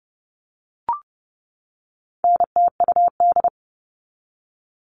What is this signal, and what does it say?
Morse code, heard as a keyed tone, about 30 wpm, 700 Hz. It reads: NTUD
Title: All Cut Numbers - 4-Digit 30wpm